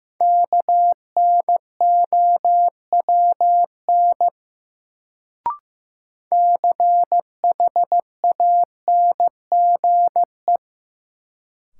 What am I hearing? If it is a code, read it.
Morse code, heard as a keyed tone, about 15 wpm, 700 Hz. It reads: KNOWN CHANGE
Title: Sentences from Top 600 Words - New Word Drill - 15wpm